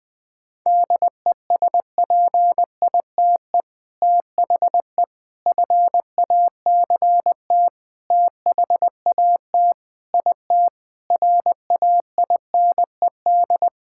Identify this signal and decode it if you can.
Morse code, heard as a keyed tone, about 20 wpm, 700 Hz. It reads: DESPITE THE FACT THAT IT RAINED
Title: Sentences with 4-Word Phrases - 20wpm